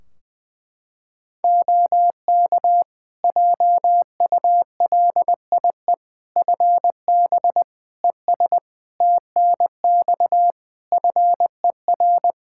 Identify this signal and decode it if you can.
Morse code, heard as a keyed tone, about 20 wpm, 700 Hz. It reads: OK JULIE FB ES TNX FER
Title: Practice QSO - Exchange 2 - Info - 20wpm